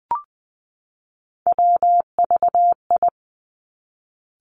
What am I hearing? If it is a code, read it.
Morse code, heard as a keyed tone, about 20 wpm, 700 Hz. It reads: WVI
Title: Three Challenging Letter-Number Combinations 20wpm